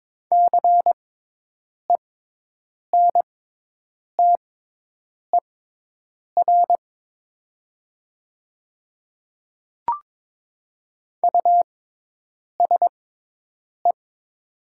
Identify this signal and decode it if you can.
Morse code, heard as a keyed tone, about 22 wpm, spaced as for 8 wpm, 700 Hz. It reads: CENTER USE
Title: 6x Character Spacing - Top 500 Words - 22wpm